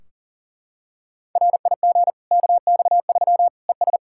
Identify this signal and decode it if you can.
Morse code, heard as a keyed tone, about 40 wpm, 700 Hz. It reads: RIG KX3 ES
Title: Practice QSO - Exchange 2 - Info - 40wpm